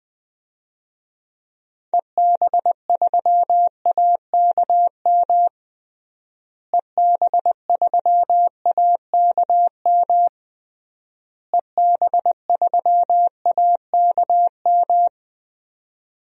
Morse code, heard as a keyed tone, about 20 wpm, 700 Hz. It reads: EB3AKM EB3AKM EB3AKM
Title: DXCC-100 Call Signs - 6 Characters - Repeated 3x - 20wpm